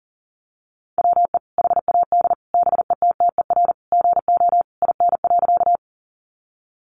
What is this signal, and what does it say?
Morse code, heard as a keyed tone, about 40 wpm, 700 Hz. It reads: WE HAD BETTER GO IN.
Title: Sentences from Top 400 Words 40wpm